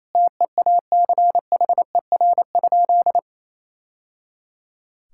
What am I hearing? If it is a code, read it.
Morse code, heard as a keyed tone, about 28 wpm, 700 Hz. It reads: TEACHER?
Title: Sentences from Top 1000 Words - Review Part 6 - 28wpm